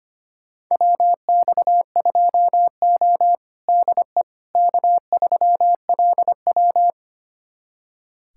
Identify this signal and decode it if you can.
Morse code, heard as a keyed tone, about 25 wpm, 700 Hz. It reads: WX2O DE K3LW
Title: USA Callsign DE Callsign - 4 Characters - 25wpm